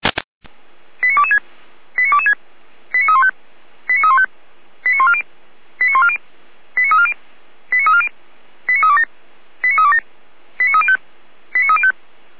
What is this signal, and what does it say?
Five-tone selective call (ZVEI), heard as quick runs of five tones, tones of about 70 ms. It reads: 89226 89226 89315 89315 89140 89140 89340 89340 89327 89327 89305 89305
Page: /455/SDR/ZVEI/